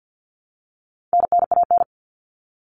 Morse code, heard as a keyed tone, about 50 wpm, 700 Hz. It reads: NNAN
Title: All Cut Numbers - 4-Digit 50wpm